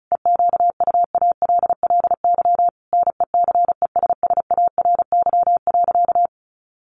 Morse code, heard as a keyed tone, about 34 wpm, 700 Hz. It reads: EQUALLY NECESSARY.
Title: Sentences from Top 1400 Words 35wpm